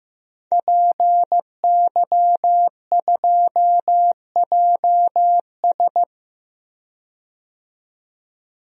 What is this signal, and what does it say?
Morse code, heard as a keyed tone, about 16 wpm, 700 Hz. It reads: PY2JS
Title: DXCC-100 Call Signs - 5 Characters - 15wpm